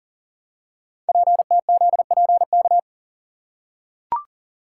Morse code, heard as a keyed tone, about 38 wpm, 690 Hz. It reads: PTZPK
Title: Code Groups - 5 Letters - 40wpm